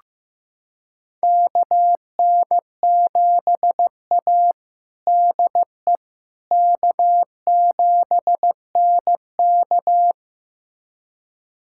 Morse code, heard as a keyed tone, about 15 wpm, 700 Hz. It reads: KN7A DE K7NK
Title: USA Callsign DE Callsign - 4 Characters - 15wpm